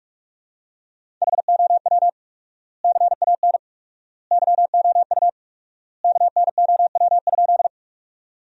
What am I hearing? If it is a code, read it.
Morse code, heard as a keyed tone, about 47 wpm, 700 Hz. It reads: HOW CAN YOU KNOW?
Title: 4x Word-Spacing - Sentences from Top 100 Words - 45wpm